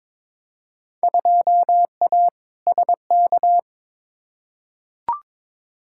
Morse code, heard as a keyed tone, about 22 wpm, 700 Hz. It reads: 2A SK
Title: Field Day Exchanges Encore - 22wpm